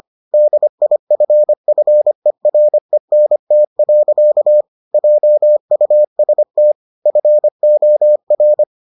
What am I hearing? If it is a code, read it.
Morse code, heard as a keyed tone, about 25 wpm, 600 Hz. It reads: DIFFERENT. JUST FOR